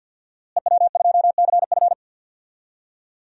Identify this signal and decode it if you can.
Morse code, heard as a keyed tone, about 53 wpm, 700 Hz. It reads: EW2CF
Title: DXCC-100 Call Signs - 5 Characters - Repeated 3x - 50wpm